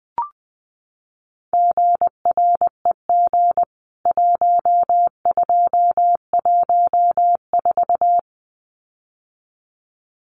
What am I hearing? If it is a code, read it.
Morse code, heard as a keyed tone, about 20 wpm, 700 Hz. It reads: GREG 1214
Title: CWT Exchange 20wpm